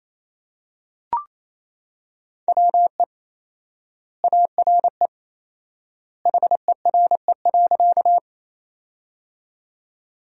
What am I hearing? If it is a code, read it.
Morse code, heard as a keyed tone, about 28 wpm, 700 Hz. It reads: WE ARE HERE.
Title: 4x Word-Spacing - Sentences from Top 200 Words - 28wpm